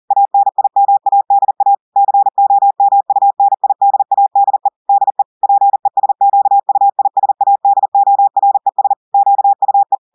Morse code, heard as a keyed tone, about 40 wpm, 800 Hz. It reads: ANIMADA COMUNIDADE DE PESQUISADORES QUE
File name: audios / portugais / portugais texte en clair 1 vitesse 40 mots minutes